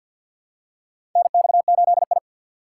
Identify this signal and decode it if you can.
Morse code, heard as a keyed tone, about 51 wpm, 700 Hz. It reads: NX8I
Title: USA Call Signs - 4 Characters - 2x Character Spacing - 50wpm